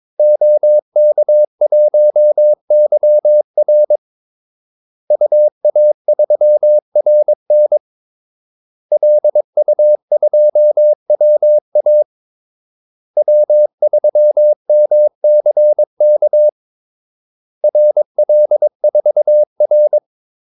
Morse code, heard as a keyed tone, about 22 wpm, 600 Hz. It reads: OK1YR UA3RN LU2WA W3MCK RL4R
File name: Callsigns-22wpm